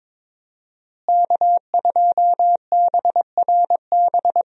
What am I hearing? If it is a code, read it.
Morse code, heard as a keyed tone, about 22 wpm, 700 Hz. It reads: K2BRB
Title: USA Call Signs - 5 Characters - Repeated 2x - 22wpm